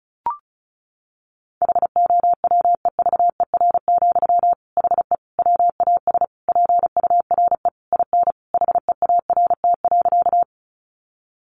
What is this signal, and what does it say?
Morse code, heard as a keyed tone, about 34 wpm, 700 Hz. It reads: HOWEVER, HE WAS PURE IN HEART.
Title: Sentences from Top 1500 Words 35wpm